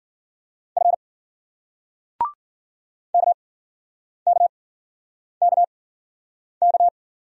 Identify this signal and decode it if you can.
Morse code, heard as a keyed tone, about 53 wpm, spaced as for 9 wpm, 700 Hz. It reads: F KKKK